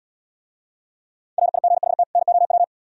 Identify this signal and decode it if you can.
Morse code, heard as a keyed tone, about 73 wpm, 700 Hz. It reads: RIGHT NOW